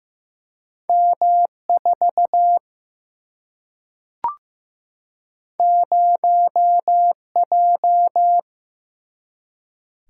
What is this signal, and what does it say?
Morse code, heard as a keyed tone, about 15 wpm, 700 Hz. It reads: M 4 0 J
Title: Single-Letter Single-Number 15wpm